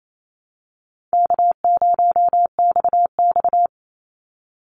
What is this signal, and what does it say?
Morse code, heard as a keyed tone, about 28 wpm, 700 Hz. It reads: K0XX
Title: Easy Call Signs Repeated 3x - 28wpm